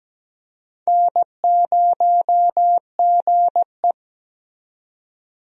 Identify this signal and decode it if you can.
Morse code, heard as a keyed tone, about 17 wpm, 700 Hz. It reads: N0GE
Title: USA Call Signs - 4 Characters - 17wpm